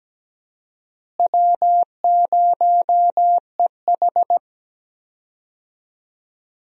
Morse code, heard as a keyed tone, about 17 wpm, 700 Hz. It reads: W0EH
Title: USA Call Signs - 4 Characters - Repeated 3x - 17wpm